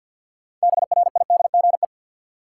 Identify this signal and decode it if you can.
Morse code, heard as a keyed tone, about 51 wpm, 700 Hz. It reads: BRIDGE
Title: Sentences from Top 1200 Words - New Word Drill Alt - 50wpm